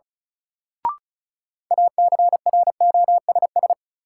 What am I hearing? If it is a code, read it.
Morse code, heard as a keyed tone, about 36 wpm, 700 Hz. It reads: ACROSS
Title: Sentences from Top 700 Words - New Word Drill - 35wpm